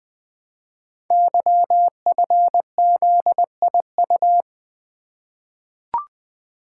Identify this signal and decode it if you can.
Morse code, heard as a keyed tone, about 20 wpm, 700 Hz. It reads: YFZIU
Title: Code Groups - 5 Letters - 20wpm